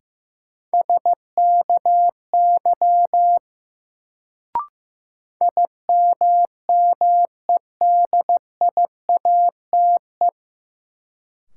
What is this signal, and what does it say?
Morse code, heard as a keyed tone, about 15 wpm, 700 Hz. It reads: SKY IMMEDIATE
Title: Sentences from Top 1200 Words - New Word Drill - 15wpm